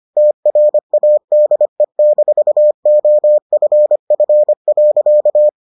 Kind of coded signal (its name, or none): Morse code